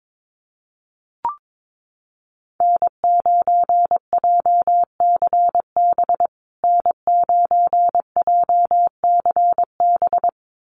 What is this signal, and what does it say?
Morse code, heard as a keyed tone, about 22 wpm, 700 Hz. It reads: N9JCB N9JCB
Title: USA Call Signs Repeated 2x 22wpm